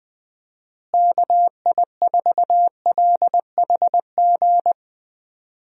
Morse code, heard as a keyed tone, about 20 wpm, 700 Hz. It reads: KI4LHG
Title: USA Call Signs - 6 Characters - Repeated 2x - 20wpm